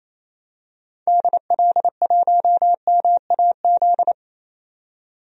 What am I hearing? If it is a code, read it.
Morse code, heard as a keyed tone, about 28 wpm, 700 Hz. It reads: DL1MAZ
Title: DXCC-100 Call Signs - 6 Characters - 28wpm